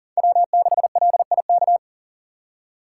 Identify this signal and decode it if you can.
Morse code, heard as a keyed tone, about 40 wpm, 700 Hz. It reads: W6LIK
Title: USA Call Signs - 4x Character Spacing - 40wpm